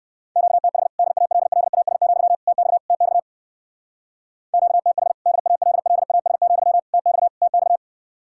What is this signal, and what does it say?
Morse code, heard as a keyed tone, about 64 wpm, 700 Hz. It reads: QTH DALLAS, TX TX QTH DALLAS, TX TX